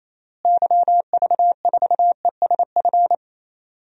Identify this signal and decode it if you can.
Morse code, heard as a keyed tone, about 28 wpm, 700 Hz. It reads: YV4ESF
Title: DXCC-100 Call Signs - 6 Characters - Repeated 3x - 28wpm